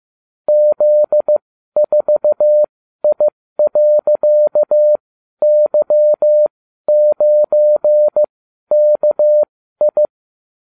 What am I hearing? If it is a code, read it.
Morse code, heard as a keyed tone, about 15 wpm, spaced as for 12 wpm, 600 Hz. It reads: Z4I.Y9KI